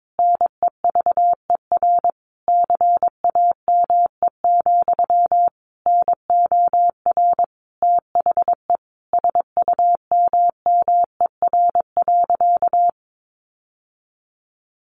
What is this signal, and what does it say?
Morse code, heard as a keyed tone, about 22 wpm, 700 Hz. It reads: NEVER CAME, NOR THE SUMMER.
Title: Sentences from Top 1500 Words 22wpm